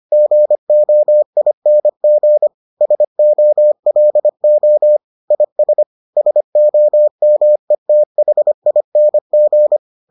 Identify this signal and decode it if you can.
Morse code, heard as a keyed tone, about 25 wpm, 600 Hz. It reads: GOING SOLO IS SOMETHING